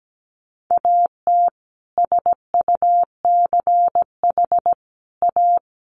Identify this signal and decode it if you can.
Morse code, heard as a keyed tone, about 17 wpm, 700 Hz. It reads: AT SUCH A
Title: Sentences from Top 1000 Words 17wpm